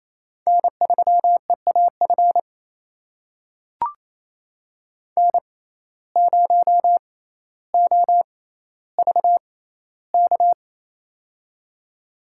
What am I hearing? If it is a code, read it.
Morse code, heard as a keyed tone, about 28 wpm, spaced as for 14 wpm, 700 Hz. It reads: N3EAF N0OVK
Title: USA Call Signs - 5 Characters - 6x Character Spacing - 28wpm